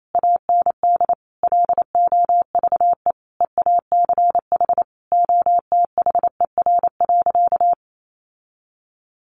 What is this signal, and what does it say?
Morse code, heard as a keyed tone, about 28 wpm, 700 Hz. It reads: AND LOVE EACH OTHER.
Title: Sentences from Top 1500 Words 28wpm